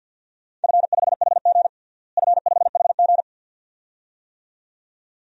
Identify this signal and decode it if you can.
Morse code, heard as a keyed tone, about 50 wpm, 700 Hz. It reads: F5HG F5HG
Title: DXCC-100 Call Signs - 4 Characters - Repeated 2x - 50wpm